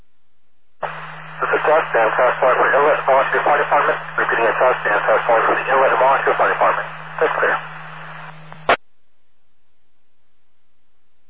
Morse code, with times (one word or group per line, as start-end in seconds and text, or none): none